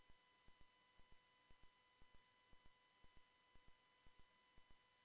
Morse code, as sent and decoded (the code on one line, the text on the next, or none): none